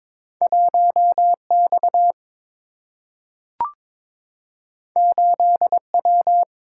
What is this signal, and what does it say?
Morse code, heard as a keyed tone, about 22 wpm, 700 Hz. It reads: 1 X 8 W
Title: Single-Letter Single-Number 22wpm